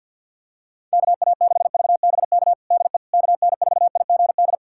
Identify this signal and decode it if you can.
Morse code, heard as a keyed tone, about 47 wpm, 700 Hz. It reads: KA6VBK DE KN4IGD
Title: USA Callsign DE Callsign 50wpm